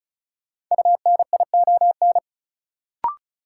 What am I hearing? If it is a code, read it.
Morse code, heard as a keyed tone, about 36 wpm, 700 Hz. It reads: UNION
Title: Sentences from Top 1400 Words - New Word Drill Alt - 35wpm